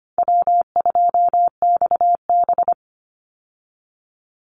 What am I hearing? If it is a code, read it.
Morse code, heard as a keyed tone, about 25 wpm, 700 Hz. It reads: W2XB
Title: Easy Call Signs - 25wpm